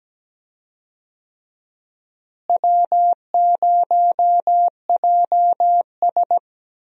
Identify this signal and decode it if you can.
Morse code, heard as a keyed tone, about 17 wpm, 700 Hz. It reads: W0JS
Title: USA Call Signs - 4 Characters - Repeated 3x - 17wpm